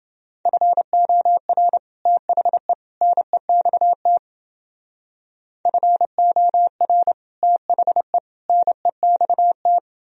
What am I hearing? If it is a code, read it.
Morse code, heard as a keyed tone, about 29 wpm, 700 Hz. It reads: FOR THE NEXT FOR THE NEXT